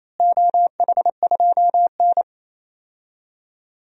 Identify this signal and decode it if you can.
Morse code, heard as a keyed tone, about 28 wpm, 700 Hz. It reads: OH2N
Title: DXCC-100 Call Signs - 4 Characters - Repeated 2x - 28wpm